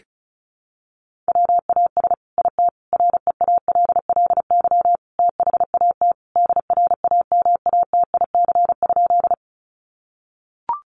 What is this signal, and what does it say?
Morse code, heard as a keyed tone, about 34 wpm, 700 Hz. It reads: WAS IT REALLY THAT DRAMATIC?
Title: Sentences from Top 1300 Words 35wpm